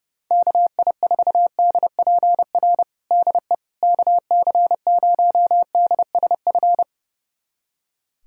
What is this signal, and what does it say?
Morse code, heard as a keyed tone, about 32 wpm, 700 Hz. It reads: KI4DPR DE KC0DSF
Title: USA Callsign DE Callsign 30wpm